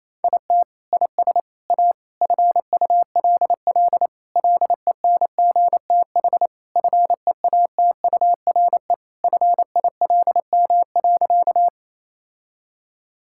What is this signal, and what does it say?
Morse code, heard as a keyed tone, about 28 wpm, 700 Hz. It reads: IT IS A FULL LENGTH FEATURE FILM.
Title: Sentences from Top 1000 Words - Review Part 5 - 28wpm